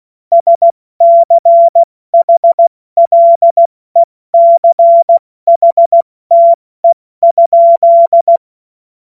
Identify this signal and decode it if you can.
Morse code, heard as a keyed tone, about 16 wpm, 680 Hz. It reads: SCHLECHTE?